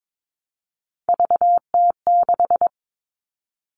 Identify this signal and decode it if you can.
Morse code, heard as a keyed tone, about 22 wpm, 700 Hz. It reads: VT6
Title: All Cut Numbers - 3-Digit 22wpm